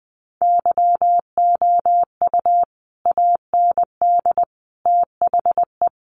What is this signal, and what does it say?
Morse code, heard as a keyed tone, about 20 wpm, 700 Hz. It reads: YOU AND THE
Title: Sentences from Top 1300 Words 20wpm